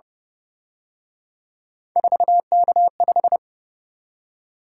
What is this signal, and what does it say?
Morse code, heard as a keyed tone, about 31 wpm, 700 Hz. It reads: 4K5